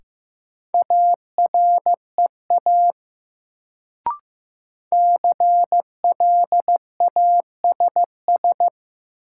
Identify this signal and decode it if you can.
Morse code, heard as a keyed tone, about 15 wpm, 700 Hz. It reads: AREA CLASS
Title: Sentences from Top 1000 Words - Review Part 2 - New Word Drill - 15wpm